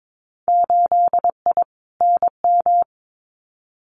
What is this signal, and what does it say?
Morse code, heard as a keyed tone, about 22 wpm, 700 Hz. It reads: 8I NM
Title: Winter Field Day Exchanges Encore 22wpm